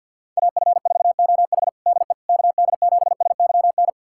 Morse code, heard as a keyed tone, about 51 wpm, 700 Hz. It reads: AF4OH DE KD7SYN
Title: USA Callsign DE Callsign 50wpm